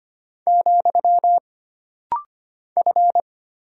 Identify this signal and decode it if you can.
Morse code, heard as a keyed tone, about 25 wpm, 700 Hz. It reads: , F
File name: Single Character (Letter:Number:Punctuation:Prosign) - Rapid-Fire - 25wpm